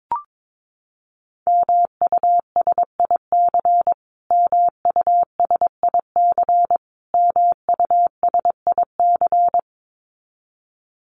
Morse code, heard as a keyed tone, about 22 wpm, 700 Hz. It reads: MUSIC MUSIC MUSIC
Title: Top 500 Words Repeated 22wpm